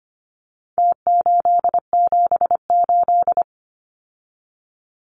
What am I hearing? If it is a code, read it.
Morse code, heard as a keyed tone, about 25 wpm, 700 Hz. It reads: T878
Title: Numbers - Contest Style - 4-Digit 25wpm